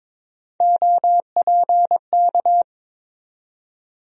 Morse code, heard as a keyed tone, about 22 wpm, 700 Hz. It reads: OPK
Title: USA Call Sign Postfixes - 3 Letters 22wpm